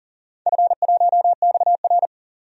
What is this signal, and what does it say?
Morse code, heard as a keyed tone, about 40 wpm, 700 Hz. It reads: F1XR
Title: DXCC-100 Call Signs - 4 Characters - 40wpm